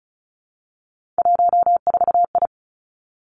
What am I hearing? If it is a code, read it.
Morse code, heard as a keyed tone, about 36 wpm, 700 Hz. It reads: 14I
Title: Three Challenging Letter-Number Combinations 35wpm